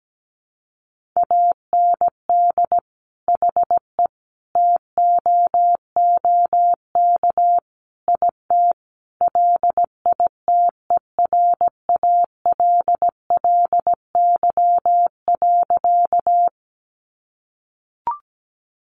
Morse code, heard as a keyed tone, about 17 wpm, 700 Hz. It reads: AND HE TOOK IT LITERALLY.
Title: Sentences from Top 1400 Words 17wpm